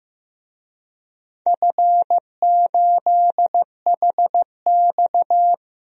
Morse code, heard as a keyed tone, about 15 wpm, 700 Hz. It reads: F8HX
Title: DXCC-100 Call Signs - 4 Characters - Repeated 2x - 15wpm